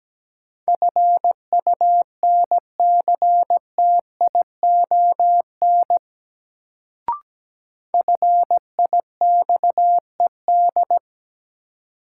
Morse code, heard as a keyed tone, about 17 wpm, 700 Hz. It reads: FUNCTION FIXED
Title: Sentences from Top 1100 Words - New Word Drill - 17wpm